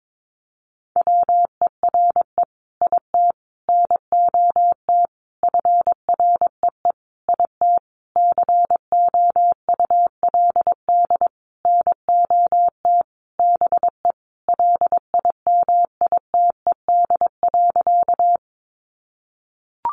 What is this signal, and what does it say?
Morse code, heard as a keyed tone, about 22 wpm, 700 Hz. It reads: WERE IT NOT FREE IT COULD NOT BE LIMITED.
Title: Sentences from Top 900 Words 22wpm